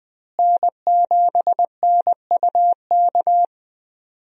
Morse code, heard as a keyed tone, about 20 wpm, 700 Hz. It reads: N7NUK
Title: USA Call Signs - 5 Characters - Repeated 3x - 20wpm